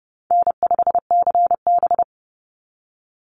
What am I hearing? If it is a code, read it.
Morse code, heard as a keyed tone, about 30 wpm, 700 Hz. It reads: N5CB
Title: USA Call Signs 30wpm